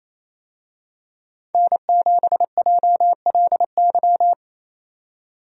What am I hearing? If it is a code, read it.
Morse code, heard as a keyed tone, about 28 wpm, 700 Hz. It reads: N7JLY
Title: USA Call Signs - 5 Characters - 28wpm